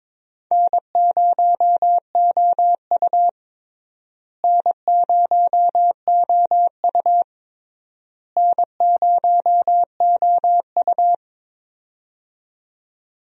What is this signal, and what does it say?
Morse code, heard as a keyed tone, about 22 wpm, 700 Hz. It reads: N0OU N0OU N0OU
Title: USA Call Signs - 4 Characters - Repeated 3x - 22wpm